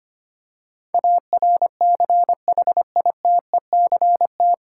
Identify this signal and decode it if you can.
Morse code, heard as a keyed tone, about 25 wpm, 700 Hz. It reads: ARCHITECT
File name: Nine Letter Words - 25wpm